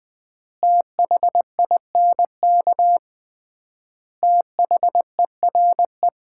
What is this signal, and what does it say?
Morse code, heard as a keyed tone, about 21 wpm, 700 Hz. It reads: THINK THERE
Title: 3x Word-Spacing - Sets of 4 Words - 20wpm